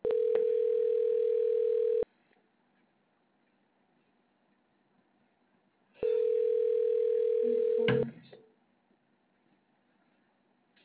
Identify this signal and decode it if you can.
Ringback tone (North America), 440 + 480 Hz.